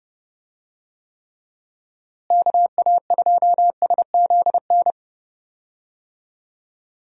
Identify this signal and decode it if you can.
Morse code, heard as a keyed tone, about 31 wpm, 700 Hz. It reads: KA2SZN
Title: USA Call Signs - 6 Characters - Repeated 3x - 30wpm